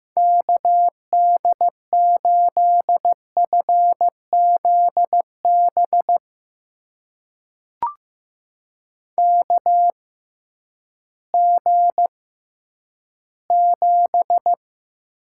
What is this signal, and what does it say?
Morse code, heard as a keyed tone, about 15 wpm, spaced as for 7 wpm, 700 Hz. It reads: KD8FZ BKG7
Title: USA Call Signs - 6x Character Spacing - 15wpm